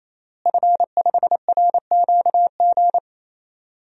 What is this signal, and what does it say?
Morse code, heard as a keyed tone, about 28 wpm, 700 Hz. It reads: F5RQG
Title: DXCC-100 Call Signs - 5 Characters - Repeated 3x - 28wpm